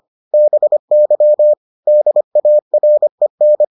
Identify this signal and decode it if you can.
Morse code, heard as a keyed tone, about 25 wpm, 600 Hz. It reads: BY DAREN